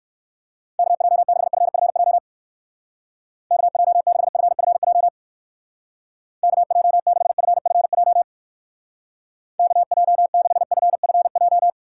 Morse code, heard as a keyed tone, about 55 wpm, 700 Hz. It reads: KJ6FFJ KJ6FFJ KJ6FFJ KJ6FFJ
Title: Speed Racing - USA 2x3 Call Signs 45wpm